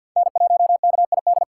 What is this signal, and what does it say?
Morse code, heard as a keyed tone, about 53 wpm, 700 Hz. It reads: N1KID
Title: USA Call Signs - 3x Character Spacing - 50wpm